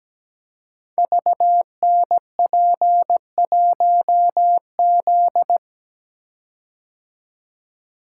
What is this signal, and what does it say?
Morse code, heard as a keyed tone, about 17 wpm, 700 Hz. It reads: VNP1Z
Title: Code Groups - 5 Letter:Numbers - 17wpm